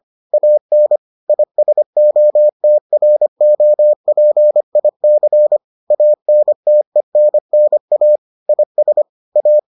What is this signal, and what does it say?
Morse code, heard as a keyed tone, about 25 wpm, 600 Hz. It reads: AN ISOTROPIC ANTENNA IS A